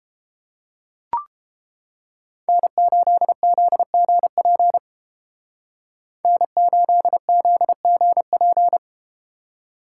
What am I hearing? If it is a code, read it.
Morse code, heard as a keyed tone, about 30 wpm, 700 Hz. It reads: N8ZGP N8ZGP